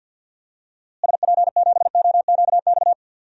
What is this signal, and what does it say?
Morse code, heard as a keyed tone, about 50 wpm, 700 Hz. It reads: SP7OQX
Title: DXCC-100 Call Signs - 6 Characters - Repeated 2x - 50wpm